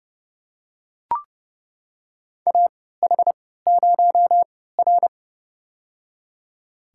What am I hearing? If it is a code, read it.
Morse code, heard as a keyed tone, about 30 wpm, spaced as for 17 wpm, 700 Hz. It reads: AH0R
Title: USA Call Signs - 4 Characters - 3x Character Spacing - 30wpm